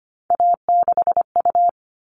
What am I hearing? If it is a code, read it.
Morse code, heard as a keyed tone, about 25 wpm, 700 Hz. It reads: A6U
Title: All Cut Numbers - 3-Digit 25wpm